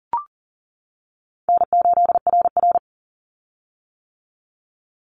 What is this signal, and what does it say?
Morse code, heard as a keyed tone, about 40 wpm, 700 Hz. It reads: N8RR
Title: Call Signs - 40wpm